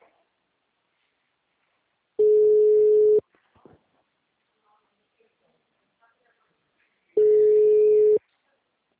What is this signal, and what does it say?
Ringback tone (Europe, 425 Hz), with speech.
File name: out- -2012- - - .354